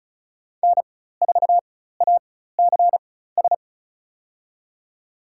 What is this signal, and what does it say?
Morse code, heard as a keyed tone, about 36 wpm, spaced as for 16 wpm, 700 Hz. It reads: N4ACS
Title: USA Call Signs - 5 Characters - 4x Character Spacing - 35wpm